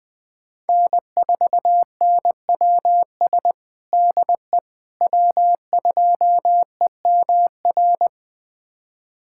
Morse code, heard as a keyed tone, about 21 wpm, 700 Hz. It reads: N4NWS DE W2EMR
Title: USA Callsign DE Callsign - 5 Characters - 20wpm